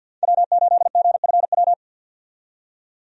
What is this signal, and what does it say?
Morse code, heard as a keyed tone, about 52 wpm, 700 Hz. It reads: W8GFW
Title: USA Call Signs - 5 Characters - 50wpm